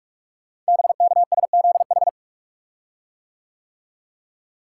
Morse code, heard as a keyed tone, about 44 wpm, 690 Hz. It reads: BKSZH
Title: Code Groups - 5 Letters - 45wpm